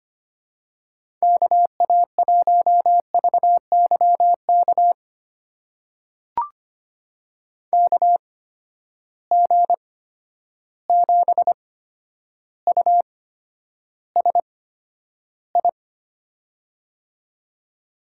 Morse code, heard as a keyed tone, about 25 wpm, spaced as for 11 wpm, 700 Hz. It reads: KA1VYK KG7USI
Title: USA Call Signs - 6 Characters - 8x Character Spacing - 25wpm